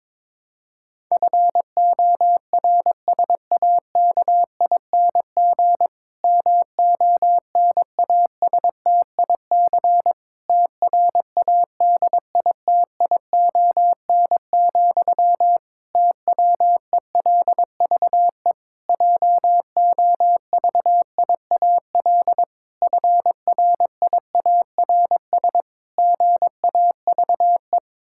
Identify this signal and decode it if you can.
Morse code, heard as a keyed tone, about 22 wpm, 700 Hz. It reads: FORSAKING MONASTIC TRADITION, TWELVE JOVIAL FRIARS GAVE
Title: Pangram Challenge v2 - 22wpm